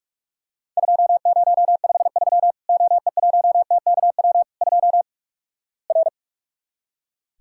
Morse code, heard as a keyed tone, about 46 wpm, 700 Hz. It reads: 2053 OE1TKW 2 R
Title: Worked All Europe DX Contest - QTC Exchange - 45wpm